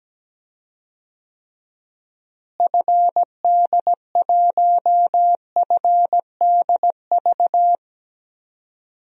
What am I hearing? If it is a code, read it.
Morse code, heard as a keyed tone, about 17 wpm, 700 Hz. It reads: FD1FDV
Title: DXCC-100 Call Signs - 6 Characters - Repeated 3x - 17wpm